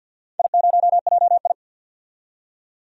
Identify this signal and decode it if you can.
Morse code, heard as a keyed tone, about 47 wpm, 700 Hz. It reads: I0JI
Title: DXCC-100 Call Signs - 4 Characters - 50wpm